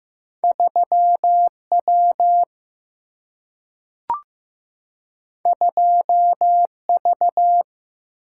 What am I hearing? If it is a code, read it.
Morse code, heard as a keyed tone, about 15 wpm, 700 Hz. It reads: 3 W 2 V
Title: Single-Letter Single-Number 15wpm